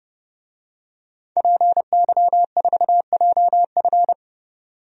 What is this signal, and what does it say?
Morse code, heard as a keyed tone, about 31 wpm, 700 Hz. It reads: PY4JF
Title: DXCC-100 Call Signs - 5 Characters - 30wpm